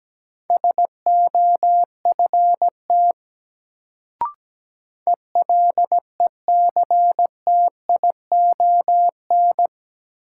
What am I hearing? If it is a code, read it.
Morse code, heard as a keyed tone, about 17 wpm, 700 Hz. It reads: SOFT ELECTION
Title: Sentences from Top 1500 Words - New Word Drill Alt - 17wpm